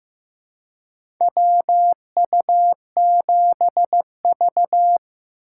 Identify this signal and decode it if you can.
Morse code, heard as a keyed tone, about 15 wpm, 700 Hz. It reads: WU7V
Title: USA Call Signs - 4 Characters - 4x Character Spacing - 15wpm